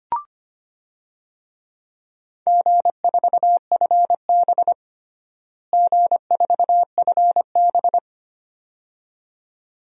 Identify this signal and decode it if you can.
Morse code, heard as a keyed tone, about 25 wpm, 700 Hz. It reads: G4FB G4FB
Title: DXCC-100 Call Signs - 4 Characters - Repeated 2x - 25wpm